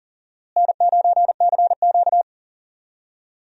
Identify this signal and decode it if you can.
Morse code, heard as a keyed tone, about 40 wpm, 700 Hz. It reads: N9CQ
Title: USA Call Signs - 4 Characters - Repeated 2x - 40wpm